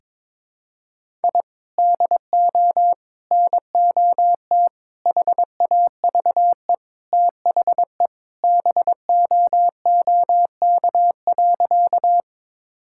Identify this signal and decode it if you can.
Morse code, heard as a keyed tone, about 22 wpm, 700 Hz. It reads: I DO NOT HAVE THE BOOK.
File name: practice / 22wpm / Sentences with 4-Word Phrases - 22wpm